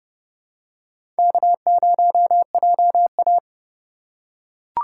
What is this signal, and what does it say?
Morse code, heard as a keyed tone, about 32 wpm, 700 Hz. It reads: K0JA
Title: USA Call Signs - 4x Character Spacing - 30wpm